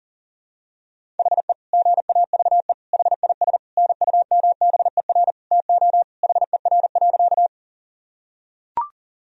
Morse code, heard as a keyed tone, about 40 wpm, 700 Hz. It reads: HE GAVE HIS NUMBER TO HER.
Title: Sentences from Top 1000 Words - Review Part 1 - 40wpm